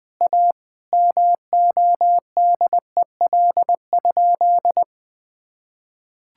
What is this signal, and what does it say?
Morse code, heard as a keyed tone, about 20 wpm, 700 Hz. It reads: A MODEL?
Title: Sentences from Top 1000 Words - Review Part 5 - 20wpm